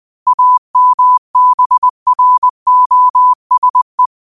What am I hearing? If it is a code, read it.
Morse code, heard as a keyed tone, about 20 wpm, 1000 Hz. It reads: AMBROSE